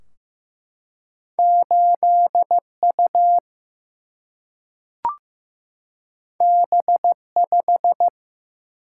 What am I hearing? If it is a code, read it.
Morse code, heard as a keyed tone, about 15 wpm, 700 Hz. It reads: 8U B5
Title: Single-Letter Single-Number 15wpm